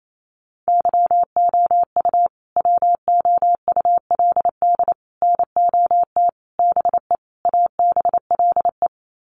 Morse code, heard as a keyed tone, about 28 wpm, 700 Hz. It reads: YOU WOULD NOT BE ABLE
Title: Sets of 5 Words Encore2 28wpm